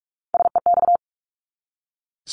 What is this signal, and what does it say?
Morse code, heard as a keyed tone, about 44 wpm, 700 Hz. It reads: SEX
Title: Top 1501-1750 Words 45wpm